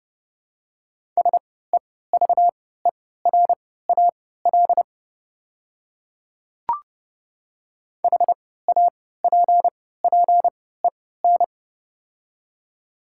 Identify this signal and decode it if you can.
Morse code, heard as a keyed tone, about 30 wpm, spaced as for 17 wpm, 700 Hz. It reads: SEVERAL HAPPEN
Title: 3x Character Spacing - Top 500 Words - 30wpm